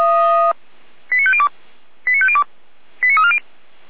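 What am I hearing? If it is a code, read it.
Five-tone selective call (ZVEI), heard as quick runs of five tones, tones of about 70 ms. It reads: 89502 89502 89340
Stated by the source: ZVEI five-tone selective call 89305 received on an SDR, from the Feuerwehr Orlamünde fire brigade server.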